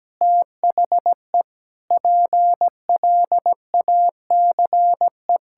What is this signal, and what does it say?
Morse code, heard as a keyed tone, about 17 wpm, 700 Hz. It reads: THE PLACE
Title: Sentences from Top 1600 Words 17wpm